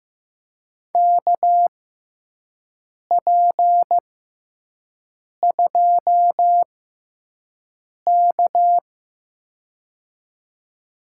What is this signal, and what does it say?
Morse code, heard as a keyed tone, about 16 wpm, spaced as for 5 wpm, 700 Hz. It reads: KP2K
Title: USA Call Signs - 4 Characters - 6x Character Spacing - 15wpm